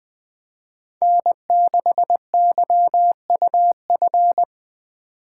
Morse code, heard as a keyed tone, about 20 wpm, 700 Hz. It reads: N6YUF
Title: Speed Racing - USA 1x3 Call Signs 17wpm